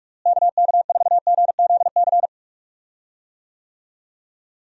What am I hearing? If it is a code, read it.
Morse code, heard as a keyed tone, about 46 wpm, 700 Hz. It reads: KK4GZC
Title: USA Call Signs - 6 Characters - Repeated 2x - 45wpm